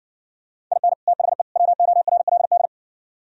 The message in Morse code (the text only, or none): IN THE WORLD